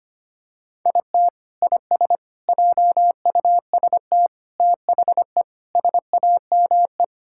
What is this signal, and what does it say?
Morse code, heard as a keyed tone, about 25 wpm, 700 Hz. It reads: IT IS JUST THE SAME
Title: Sentences from Top 1000 Words - Review Part 6 - 25wpm